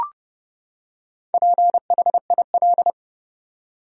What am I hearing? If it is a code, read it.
Morse code, heard as a keyed tone, about 30 wpm, 700 Hz. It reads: PHIL
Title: USA Common Names - 30wpm